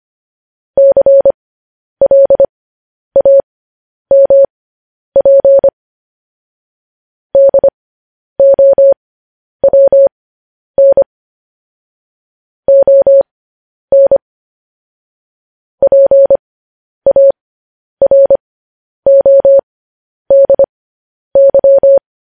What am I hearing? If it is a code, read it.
Morse code, heard as a keyed tone, about 25 wpm, spaced as for 10 wpm, 550 Hz. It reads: CLAMP DOWN ON PARODY
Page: news headlines in morse code at 10 wpm